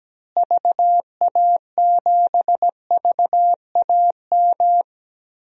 Morse code, heard as a keyed tone, about 17 wpm, 700 Hz. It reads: VA7VAM
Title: DXCC-100 Call Signs - 6 Characters - Repeated 2x - 17wpm